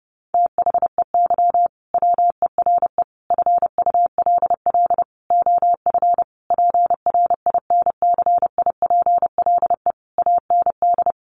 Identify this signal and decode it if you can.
Morse code, heard as a keyed tone, about 30 wpm, 700 Hz. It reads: THEY WERE FULL OF PRINCIPLE AND
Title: Sentences from Top 1400 Words 30wpm